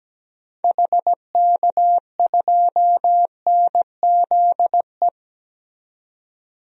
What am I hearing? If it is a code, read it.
Morse code, heard as a keyed tone, about 17 wpm, 700 Hz. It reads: HK2NZE
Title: DXCC-100 Call Signs - 6 Characters - 17wpm